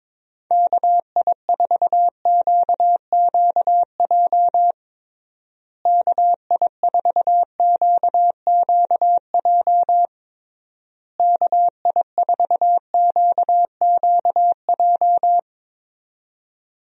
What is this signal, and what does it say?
Morse code, heard as a keyed tone, about 22 wpm, 700 Hz. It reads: KI4QQJ KI4QQJ KI4QQJ
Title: USA Call Signs - 6 Characters - Repeated 3x - 22wpm